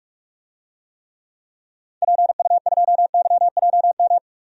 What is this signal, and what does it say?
Morse code, heard as a keyed tone, about 47 wpm, 700 Hz. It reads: PU2YJM
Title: DXCC-100 Call Signs - 6 Characters - 45wpm